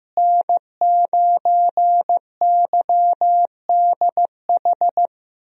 Morse code, heard as a keyed tone, about 15 wpm, 700 Hz. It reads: N9YDH
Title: USA Call Signs - 5 Characters - Repeated 3x - 15wpm